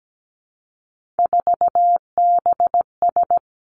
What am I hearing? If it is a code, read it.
Morse code, heard as a keyed tone, about 17 wpm, 700 Hz. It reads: 4BS
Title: Three Challenging Letter-Number Combinations 17wpm